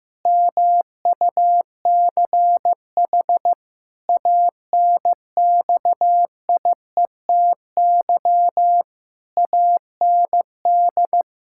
Morse code, heard as a keyed tone, about 15 wpm, 700 Hz. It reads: MUCH ANXIETY AND
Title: Pangram Challenge v2 - 15wpm